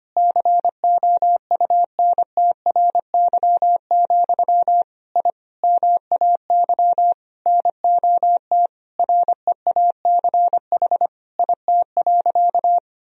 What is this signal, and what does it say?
Morse code, heard as a keyed tone, about 25 wpm, 700 Hz. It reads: COUNTRY, I MAY NOT REACH IT.
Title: Sentences from Top 1000 Words - Review Part 2 - 25wpm